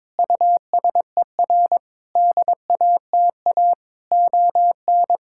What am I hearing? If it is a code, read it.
Morse code, heard as a keyed tone, about 22 wpm, 700 Hz. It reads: USER DATA ON
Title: Sentences from Top 1000 Words - Review Part 6 - 22wpm